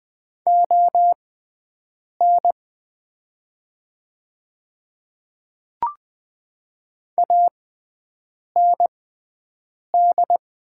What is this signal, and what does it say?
Morse code, heard as a keyed tone, about 20 wpm, spaced as for 7 wpm, 700 Hz. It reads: ON AND
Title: 6x Character Spacing - Top 100 Words - 20wpm